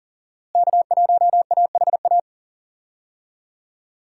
Morse code, heard as a keyed tone, about 40 wpm, 700 Hz. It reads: K1AHA
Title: USA Call Signs - 5 Characters - Repeated 3x - 40wpm